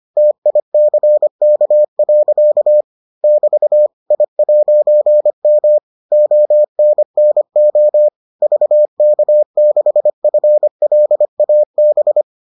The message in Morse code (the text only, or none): TICK. <BT> I'M ONNO VK6FLAB